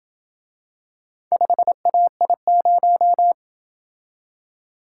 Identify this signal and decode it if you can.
Morse code, heard as a keyed tone, about 26 wpm, 700 Hz. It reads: 5AI0